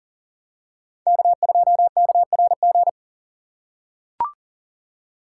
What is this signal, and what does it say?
Morse code, heard as a keyed tone, about 40 wpm, 700 Hz. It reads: K2KRG
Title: USA Call Signs - 5 Characters - 4x Character Spacing - 40wpm